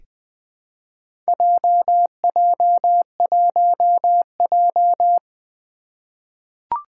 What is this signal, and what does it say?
Morse code, heard as a keyed tone, about 20 wpm, 700 Hz. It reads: JJ1J
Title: Four Challenging Letter-Number Combinations 20wpm